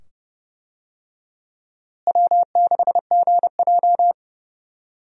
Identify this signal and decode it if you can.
Morse code, heard as a keyed tone, about 29 wpm, 700 Hz. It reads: W6GJ
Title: USA Call Signs - 4 Characters - Repeated 2x - 30wpm